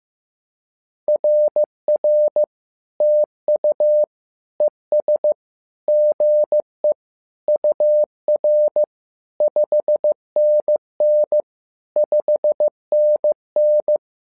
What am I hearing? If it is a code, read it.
Morse code, heard as a keyed tone, about 15 wpm, 600 Hz. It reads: RR TU ES GE UR 5NN 5NN
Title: SOTA QSO Practice 15wpm